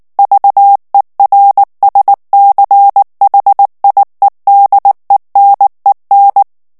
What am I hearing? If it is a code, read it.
Morse code, heard as a keyed tone, about 19 wpm, 800 Hz. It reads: VERSCHIEDENEN
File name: morse mp3 / ALLEMAND texte clair 2 vitesse 18 mots minute